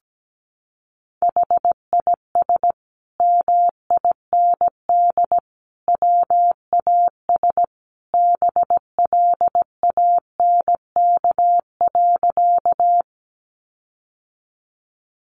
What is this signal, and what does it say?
Morse code, heard as a keyed tone, about 17 wpm, 700 Hz. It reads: HIS MIND WAS BLANK.
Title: Sentences from Top 1500 Words 17wpm